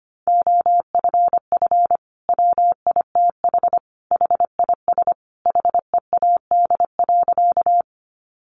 Morse code, heard as a keyed tone, about 25 wpm, 700 Hz. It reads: OFF WITH HIS HEAD.
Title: Sentences from Top 300 Words 25wpm